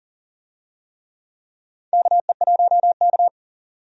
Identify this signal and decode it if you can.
Morse code, heard as a keyed tone, about 40 wpm, 690 Hz. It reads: KE1K
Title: USA Call Signs - 4 Characters - 40wpm